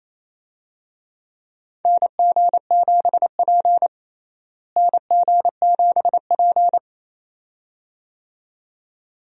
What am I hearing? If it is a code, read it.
Morse code, heard as a keyed tone, about 28 wpm, 700 Hz. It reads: NG7P NG7P
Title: USA Call Signs - 4 Characters - Repeated 2x - 28wpm